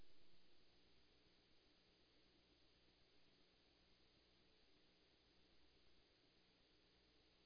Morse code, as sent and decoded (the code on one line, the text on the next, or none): none